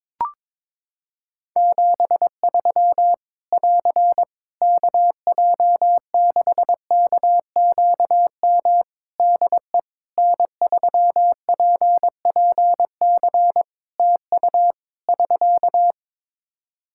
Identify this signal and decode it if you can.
Morse code, heard as a keyed tone, about 22 wpm, 700 Hz. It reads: 73 <AR> KJ6KQM DE N3PPC TU <SK>
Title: K1USN SST QSO Practice 22wpm